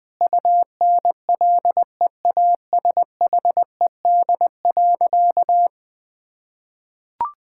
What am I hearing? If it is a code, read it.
Morse code, heard as a keyed tone, about 20 wpm, 700 Hz. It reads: UNLEASHED.
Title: Difficult Sentences with Hints - 20wpm